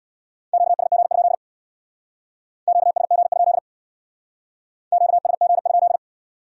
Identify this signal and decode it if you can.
Morse code, heard as a keyed tone, about 61 wpm, 700 Hz. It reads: QSG? QSG? QSG?
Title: Speed-Racing Q-Codes - 50wpm